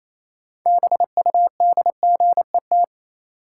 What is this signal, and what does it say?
Morse code, heard as a keyed tone, about 28 wpm, 700 Hz. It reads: BUDGET
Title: Sentences from Top 1200 Words - New Word Drill Alt - 28wpm